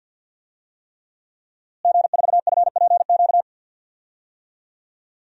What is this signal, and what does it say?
Morse code, heard as a keyed tone, about 50 wpm, 700 Hz. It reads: G4FPQ
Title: DXCC-100 Call Signs - 5 Characters - Repeated 3x - 50wpm